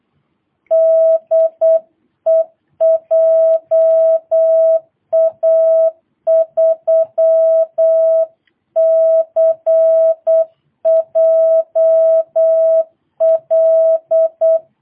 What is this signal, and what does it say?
Morse code, heard as a keyed tone, about 8 wpm, 650 Hz. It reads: DE JA3CJL
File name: jh8cbh / 2022cw